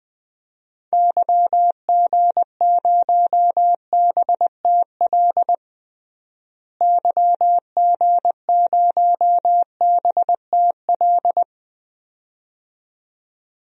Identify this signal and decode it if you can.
Morse code, heard as a keyed tone, about 20 wpm, 700 Hz. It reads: YG0BTL YG0BTL
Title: DXCC-100 Call Signs - 6 Characters - Repeated 2x - 20wpm